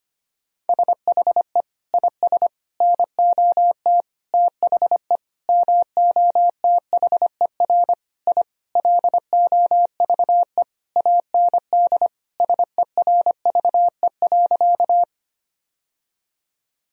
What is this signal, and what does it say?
Morse code, heard as a keyed tone, about 25 wpm, 700 Hz. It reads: SHE IS NOT THE MOTHER I LOVE AND SERVE.
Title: Sentences from Top 1000 Words - Review Part 2 - 25wpm